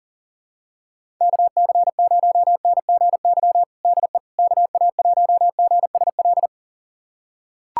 Morse code, heard as a keyed tone, about 40 wpm, 700 Hz. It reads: KC0NGY DE KA1GSL
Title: USA Callsign DE Callsign - 6 Characters - 40wpm